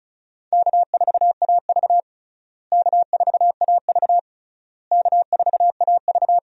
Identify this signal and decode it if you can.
Morse code, heard as a keyed tone, about 36 wpm, 700 Hz. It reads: K4AV K4AV K4AV
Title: USA Call Signs - 4 Characters - Repeated 3x - 35wpm